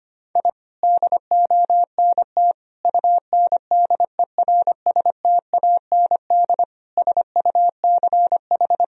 Morse code, heard as a keyed tone, about 25 wpm, 700 Hz. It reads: I DONT UNDERSTAND SUCH
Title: Sentences from Top 1000 Words - Review Part 3 - 25wpm